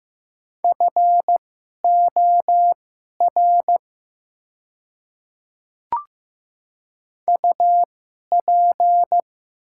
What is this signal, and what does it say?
Morse code, heard as a keyed tone, about 15 wpm, spaced as for 11 wpm, 700 Hz. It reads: FOR UP
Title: 2x Character Spacing - Top 200 Words - 15wpm